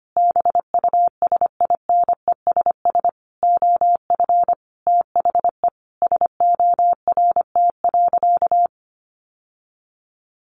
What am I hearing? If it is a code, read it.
Morse code, heard as a keyed tone, about 25 wpm, 700 Hz. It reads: BUSINESS OF THE SORT.
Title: Sentences from Top 1500 Words 25wpm